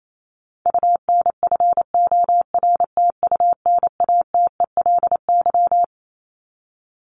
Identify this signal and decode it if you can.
Morse code, heard as a keyed tone, about 28 wpm, 700 Hz. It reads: UNFORTUNATELY
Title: LY Suffix - Top 100 Words with Base Words 28wpm